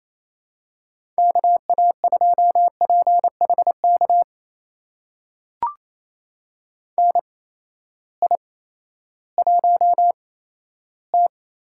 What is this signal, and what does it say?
Morse code, heard as a keyed tone, about 28 wpm, spaced as for 14 wpm, 700 Hz. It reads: KA2PHK NI1T
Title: USA Call Signs - 8x Character Spacing - 28wpm